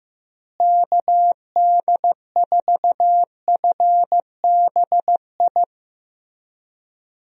Morse code, heard as a keyed tone, about 15 wpm, 700 Hz. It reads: KD4FBI
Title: USA Call Signs - 6 Characters - Repeated 3x - 15wpm